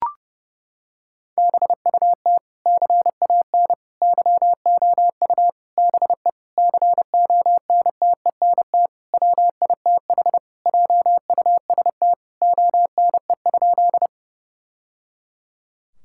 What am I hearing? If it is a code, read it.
Morse code, heard as a keyed tone, about 30 wpm, 700 Hz. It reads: BUT CAN YOU BE CONTENT WITH JUST ONE?
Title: Sentences from Top 1000 Words - Review Part 6 - 30wpm